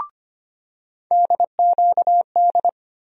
Morse code, heard as a keyed tone, about 25 wpm, 700 Hz. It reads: DQD
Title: USA Call Sign Postfixes - 3 Letters 25wpm